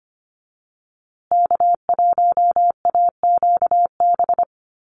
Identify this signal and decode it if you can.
Morse code, heard as a keyed tone, about 25 wpm, 700 Hz. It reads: K1AQB
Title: USA Call Signs Repeated 3x 25wpm